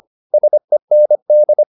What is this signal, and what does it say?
Morse code, heard as a keyed tone, about 25 wpm, 600 Hz. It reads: SEND
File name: SS315 25